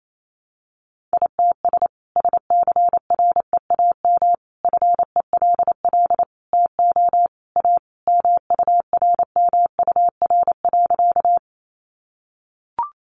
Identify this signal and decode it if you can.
Morse code, heard as a keyed tone, about 28 wpm, 700 Hz. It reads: ITS SCREAM FELL TO A MURMUR.
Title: A Sound Of Thunder 28wpm